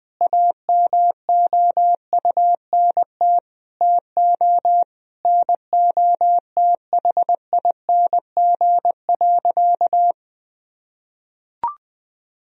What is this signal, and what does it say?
Morse code, heard as a keyed tone, about 20 wpm, 700 Hz. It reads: AMOUNT TO NOTHING.
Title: Sentences from Top 1000 Words - Review Part 4 - 20wpm